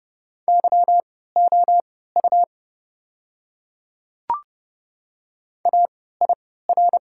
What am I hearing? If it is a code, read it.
Morse code, heard as a keyed tone, about 28 wpm, spaced as for 17 wpm, 690 Hz. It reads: YOU AIR
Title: 3x Character Spacing - Top 200 Words - 30wpm